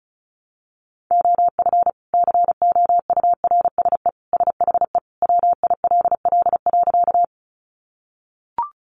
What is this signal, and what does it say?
Morse code, heard as a keyed tone, about 34 wpm, 700 Hz. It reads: OF COURSE SHE WILL.
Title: Sentences from Top 500 Words 35wpm